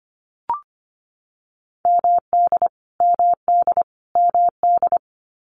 Morse code, heard as a keyed tone, about 25 wpm, 700 Hz. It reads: MD MD MD
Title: US States Repeated 3x 25wpm